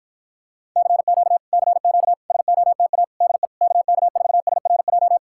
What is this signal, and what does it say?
Morse code, heard as a keyed tone, about 53 wpm, 700 Hz. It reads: CQ CQ SOTA DE KK4SRJ